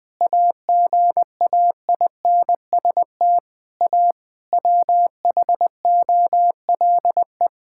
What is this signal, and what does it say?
Morse code, heard as a keyed tone, about 20 wpm, 700 Hz. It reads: AGAINST A WHOLE
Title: Sentences from Top 1000 Words - Review Part 6 - 20wpm